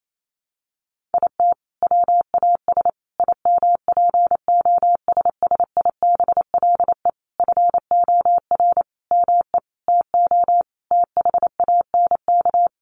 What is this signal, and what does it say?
Morse code, heard as a keyed tone, about 28 wpm, 700 Hz. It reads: IT WAS IMPOSSIBLE FOR ME TO THANK
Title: Sentences from Top 1100 Words 28wpm